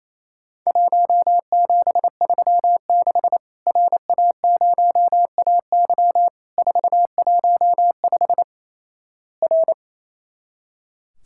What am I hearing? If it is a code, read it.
Morse code, heard as a keyed tone, about 28 wpm, 700 Hz. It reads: 1736 RA0AY 415 R
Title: Worked All Europe DX Contest - QTC Exchange - 28wpm